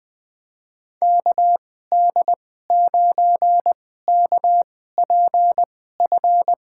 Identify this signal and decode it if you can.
Morse code, heard as a keyed tone, about 21 wpm, spaced as for 15 wpm, 700 Hz. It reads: KD9KPF
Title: USA Call Signs - 2x Character Spacing - 20wpm